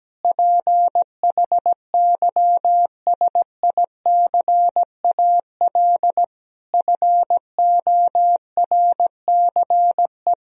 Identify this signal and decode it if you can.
Morse code, heard as a keyed tone, about 17 wpm, 700 Hz. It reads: PHYSICAL FORCE
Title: Sentences from Top 1000 Words - Review Part 4 - 17wpm